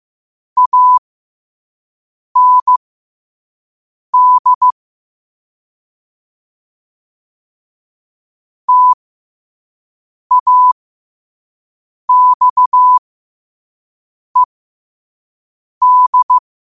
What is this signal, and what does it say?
Morse code, heard as a keyed tone, about 15 wpm, spaced as for 5 wpm, 1000 Hz. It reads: AND TAXED
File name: qotdfiles / 5wpm